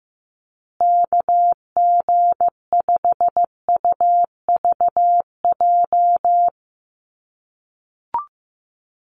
Morse code, heard as a keyed tone, about 15 wpm, 700 Hz. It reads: KG5UVJ
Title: USA Call Signs Repeated 3x 15wpm